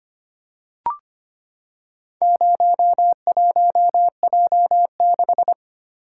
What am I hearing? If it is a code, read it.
Morse code, heard as a keyed tone, about 25 wpm, 700 Hz. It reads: 01J6
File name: practice / 25wpm / Four Letter-Number Combinations - 25wpm